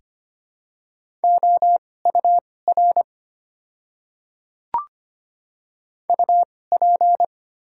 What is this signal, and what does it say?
Morse code, heard as a keyed tone, about 25 wpm, spaced as for 18 wpm, 700 Hz. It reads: OUR UP
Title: 2x Character Spacing - Top 500 Words - 25wpm